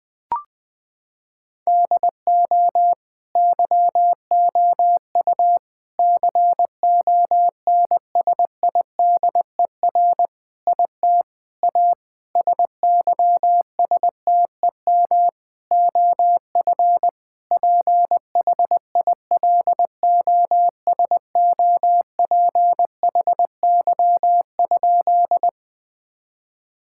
Morse code, heard as a keyed tone, about 20 wpm, 700 Hz. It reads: DO YOU CONSIDER IT A SYSTEM OF PHILOSOPHY?
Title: Sentences from Top 1000 Words - Review Part 6 - 20wpm